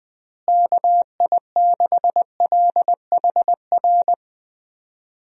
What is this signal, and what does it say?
Morse code, heard as a keyed tone, about 21 wpm, 700 Hz. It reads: KI6LHR
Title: USA Call Signs - 6 Characters - Repeated 3x - 20wpm